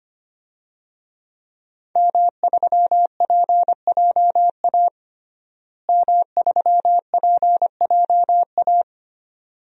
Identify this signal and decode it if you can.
Morse code, heard as a keyed tone, about 25 wpm, 700 Hz. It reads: M3PJA M3PJA
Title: DXCC-100 Call Signs - 5 Characters - Repeated 2x - 25wpm